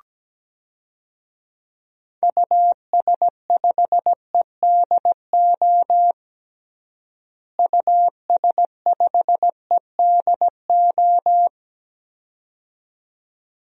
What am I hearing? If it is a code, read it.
Morse code, heard as a keyed tone, about 17 wpm, 700 Hz. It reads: US5EDO US5EDO
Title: DXCC-100 Call Signs - 6 Characters - Repeated 2x - 17wpm